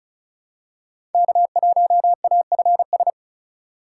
Morse code, heard as a keyed tone, about 34 wpm, 700 Hz. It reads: K1AFS
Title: USA Call Signs - 5 Characters - 3x Character Spacing - 35wpm